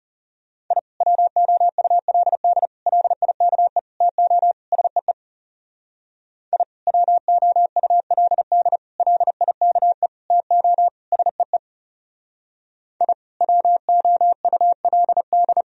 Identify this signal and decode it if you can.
Morse code, heard as a keyed tone, about 36 wpm, 700 Hz. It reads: I WOULD LIKE TO SEE I WOULD LIKE TO SEE I WOULD
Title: Speed-Racing - Sets of 5 Words 30wpm